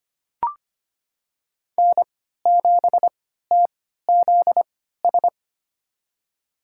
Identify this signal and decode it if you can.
Morse code, heard as a keyed tone, about 25 wpm, spaced as for 14 wpm, 700 Hz. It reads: N7TZS
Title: USA Call Signs - 5 Characters - 3x Character Spacing - 25wpm